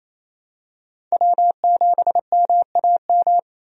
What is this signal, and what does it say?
Morse code, heard as a keyed tone, about 28 wpm, 700 Hz. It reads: W7MAM
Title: USA Call Signs - 5 Characters - Repeated 3x - 28wpm